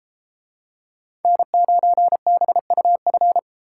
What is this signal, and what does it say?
Morse code, heard as a keyed tone, about 33 wpm, 700 Hz. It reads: N9BUF